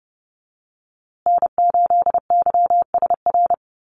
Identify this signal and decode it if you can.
Morse code, heard as a keyed tone, about 30 wpm, 700 Hz. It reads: N8YSR
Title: USA Call Signs Repeated 3x 30wpm